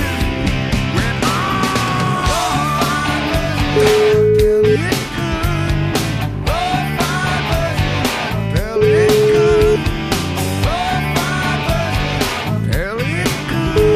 Ringback tone (Europe, 425 Hz), with noise.